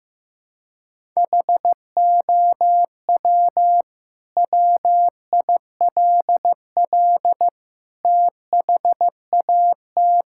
Morse code, heard as a keyed tone, about 16 wpm, 700 Hz. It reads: HOW WILL THAT
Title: Sentences from Top 1000 Words - Review Part 6 - 15wpm